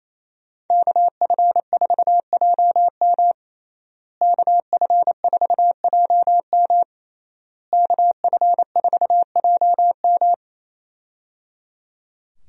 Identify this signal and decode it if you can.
Morse code, heard as a keyed tone, about 27 wpm, 700 Hz. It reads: KF4JM KF4JM KF4JM
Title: USA Call Signs - 5 Characters - Repeated 3x - 28wpm